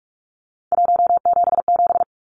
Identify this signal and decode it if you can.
Morse code, heard as a keyed tone, about 44 wpm, 700 Hz. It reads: 177
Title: Numbers - 3-Digit 45wpm